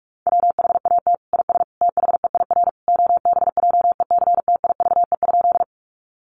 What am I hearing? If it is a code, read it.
Morse code, heard as a keyed tone, about 44 wpm, 700 Hz. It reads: WHAT IS THEIR OBJECTIVE?
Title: Sentences from Top 1200 Words 45wpm